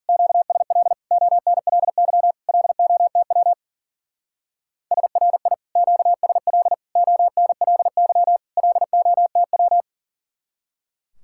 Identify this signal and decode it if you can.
Morse code, heard as a keyed tone, about 53 wpm, 700 Hz. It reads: QSL ONLY LOTW SRI QSL ONLY LOTW